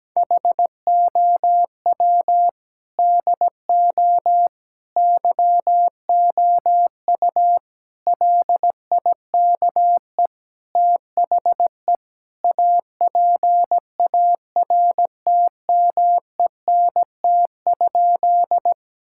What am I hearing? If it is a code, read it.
Morse code, heard as a keyed tone, about 17 wpm, 700 Hz. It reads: HOW DO YOU LIKE THE APARTMENT?
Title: Sentences from Top 1000 Words - Review Part 6 - 17wpm